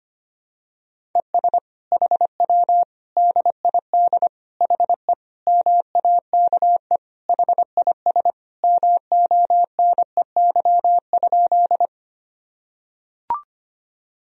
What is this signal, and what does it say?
Morse code, heard as a keyed tone, about 25 wpm, 700 Hz. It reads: ES HW DID HE MAKE HIS MONEY?
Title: Sentences from Top 400 Words and Abbreviations 25wpm